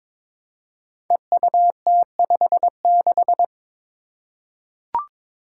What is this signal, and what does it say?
Morse code, heard as a keyed tone, about 23 wpm, 700 Hz. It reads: EUT56
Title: Code Groups - 5 Letter:Numbers - 22wpm